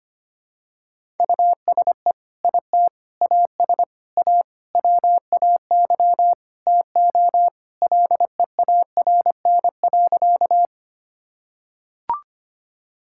Morse code, heard as a keyed tone, about 25 wpm, 700 Hz. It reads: USE IT AS A WAY TO LEARN.
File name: practice / 25wpm / Sentences with 4-Word Phrases - 25wpm